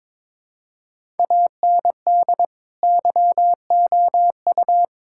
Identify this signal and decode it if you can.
Morse code, heard as a keyed tone, about 22 wpm, 700 Hz. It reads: AND YOU
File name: Build A Phrase - 2-Word Phrases - 22wpm